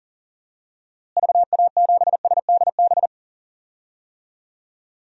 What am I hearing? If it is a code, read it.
Morse code, heard as a keyed tone, about 40 wpm, 700 Hz. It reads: VA7SDB
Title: DXCC-100 Call Signs - 6 Characters - Repeated 2x - 40wpm